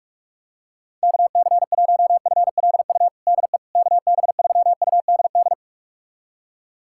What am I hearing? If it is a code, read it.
Morse code, heard as a keyed tone, about 46 wpm, 690 Hz. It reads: KC1FLU DE KB3UDD
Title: USA Callsign DE Callsign - 6 Characters - 45wpm